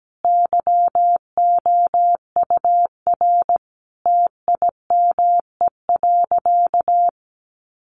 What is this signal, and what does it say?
Morse code, heard as a keyed tone, about 17 wpm, 700 Hz. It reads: YOUR TIME.
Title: Sentences from Top 100 Words 17wpm